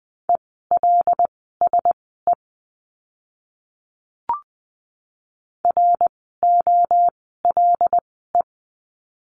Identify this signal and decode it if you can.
Morse code, heard as a keyed tone, about 20 wpm, spaced as for 14 wpm, 700 Hz. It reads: ELSE ROLE
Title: 101 Lesson 17f - Letter u - Words 20wpm